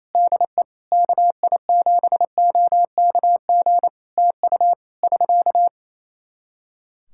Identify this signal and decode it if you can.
Morse code, heard as a keyed tone, about 28 wpm, 700 Hz. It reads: DE KI7OKG TU <SK>
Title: K1USN SST QSO Practice 28wpm